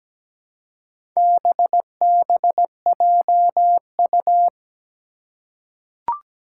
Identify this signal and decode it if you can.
Morse code, heard as a keyed tone, about 17 wpm, 700 Hz. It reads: BBJU
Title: Four Challenging Letter-Number Combinations 17wpm